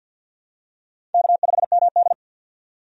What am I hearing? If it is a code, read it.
Morse code, heard as a keyed tone, about 51 wpm, 700 Hz. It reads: K5MD
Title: USA Call Signs - 4 Characters - 50wpm